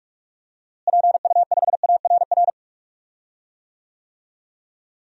Morse code, heard as a keyed tone, about 42 wpm, 700 Hz. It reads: PU5ARR
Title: DXCC-100 Call Signs - 6 Characters - Repeated 2x - 45wpm